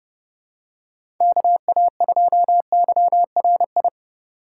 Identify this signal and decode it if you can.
Morse code, heard as a keyed tone, about 30 wpm, 700 Hz. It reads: KA2YRI
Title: USA Call Signs - 4x Character Spacing - 30wpm